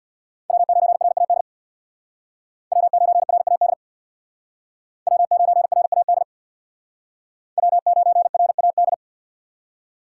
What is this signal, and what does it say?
Morse code, heard as a keyed tone, about 62 wpm, 700 Hz. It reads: W9RAD W9RAD W9RAD W9RAD
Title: Speed Racing - USA 1x3 Call Signs 50wpm